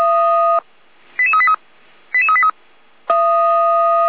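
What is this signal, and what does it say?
Five-tone selective call (ZVEI), heard as quick runs of five tones, tones of about 70 ms. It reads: 88383 88383
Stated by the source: ZVEI five-tone selective call received via SDR.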